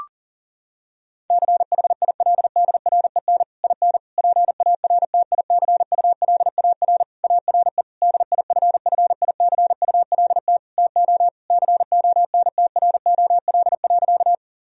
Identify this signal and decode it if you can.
Morse code, heard as a keyed tone, about 40 wpm, 700 Hz. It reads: CHILDREN IN PARTICULAR ARE DIFFICULT TO CONTROL.
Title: Sentences from Top 1000 Words - Review Part 4 - 40wpm